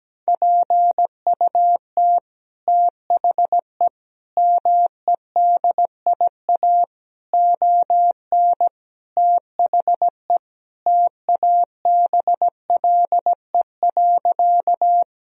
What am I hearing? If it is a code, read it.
Morse code, heard as a keyed tone, about 17 wpm, 700 Hz. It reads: PUT THE MEDIA ON THE TABLE.
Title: Sentences from Top 1000 Words - Review Part 4 - 17wpm